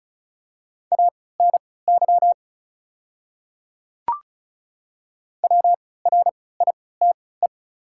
Morse code, heard as a keyed tone, about 36 wpm, spaced as for 20 wpm, 700 Hz. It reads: ANY WRITE
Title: 3x Character Spacing - Top 300 Words - 35wpm